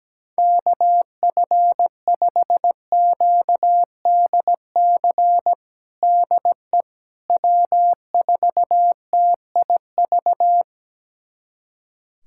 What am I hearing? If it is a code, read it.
Morse code, heard as a keyed tone, about 17 wpm, 700 Hz. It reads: KF5QDC DE W4TIV
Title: USA Callsign DE Callsign 17wpm